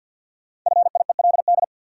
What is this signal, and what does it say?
Morse code, heard as a keyed tone, about 52 wpm, 700 Hz. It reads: FIELD
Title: Sentences from Top 1000 Words - New Word Drill - 50wpm